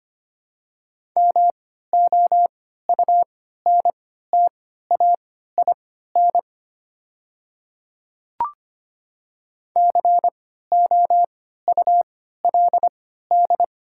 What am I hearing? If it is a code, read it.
Morse code, heard as a keyed tone, about 25 wpm, spaced as for 14 wpm, 700 Hz. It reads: MOUNTAIN COULD
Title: 3x Character Spacing - Top 300 Words - 25wpm